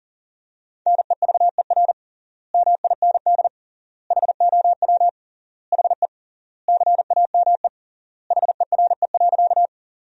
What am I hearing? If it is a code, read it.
Morse code, heard as a keyed tone, about 40 wpm, 700 Hz. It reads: NEVER MIND HOW HE CAME HERE.
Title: 3x Word-Spacing - Sentences from Top 500 Words - 40wpm